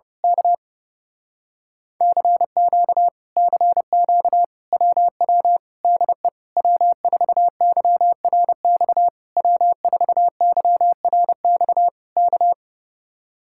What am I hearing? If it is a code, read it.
Morse code, heard as a keyed tone, about 28 wpm, 700 Hz. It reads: K CQ CQ WW DE W4YRX W4YRX K
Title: Speed-Racing Common QSO Phrases Encore - 30wpm